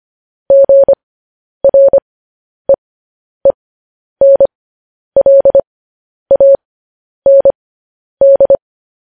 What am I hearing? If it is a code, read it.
Morse code, heard as a keyed tone, about 25 wpm, spaced as for 10 wpm, 550 Hz. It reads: GREENLAND
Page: news-headlines-in-morse-code-at-10-wpm